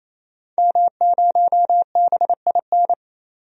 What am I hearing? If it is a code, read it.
Morse code, heard as a keyed tone, about 28 wpm, 700 Hz. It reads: M0BIN
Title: DXCC-100 Call Signs - 5 Characters - 28wpm